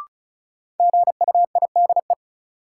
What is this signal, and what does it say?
Morse code, heard as a keyed tone, about 36 wpm, 700 Hz. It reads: GUIDE
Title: Sentences from Top 600 Words - New Word Drill - 35wpm